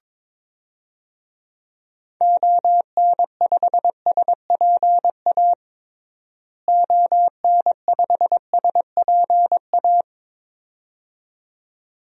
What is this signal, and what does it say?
Morse code, heard as a keyed tone, about 22 wpm, 700 Hz. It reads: ON5SPA ON5SPA
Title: DXCC-100 Call Signs - 6 Characters - Repeated 2x - 22wpm